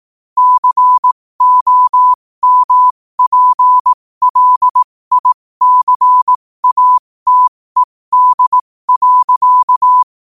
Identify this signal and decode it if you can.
Morse code, heard as a keyed tone, about 18 wpm, spaced as for 15 wpm, 1000 Hz. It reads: COMPLICATED.